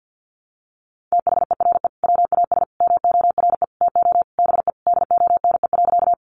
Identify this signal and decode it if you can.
Morse code, heard as a keyed tone, about 50 wpm, 700 Hz. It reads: THERE WAS MORE TO BE DONE.
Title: Sentences from Top 500 Words 50wpm